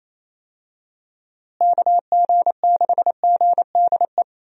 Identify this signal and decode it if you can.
Morse code, heard as a keyed tone, about 28 wpm, 700 Hz. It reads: KG6GDE
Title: USA Call Signs - 6 Characters - Repeated 2x - 28wpm